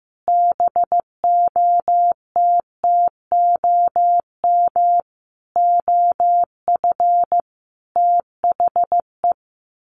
Morse code, heard as a keyed tone, about 14 wpm, 700 Hz. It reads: BOTTOM OF THE
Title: Sets of 5 Words 15wpm